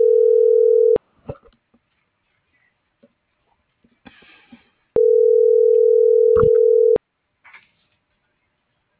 Ringback tone (North America), 440 + 480 Hz.